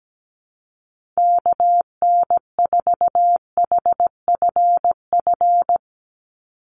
Morse code, heard as a keyed tone, about 17 wpm, 700 Hz. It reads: KN4HFF
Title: USA Call Signs 17wpm